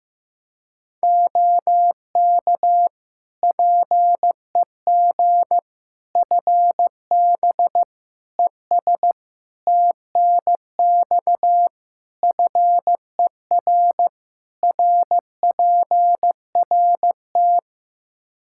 Morse code, heard as a keyed tone, about 15 wpm, 700 Hz. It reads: OK PEG FB ES TNX FER RPRT
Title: Practice QSO - Exchange 2 - Info - 15wpm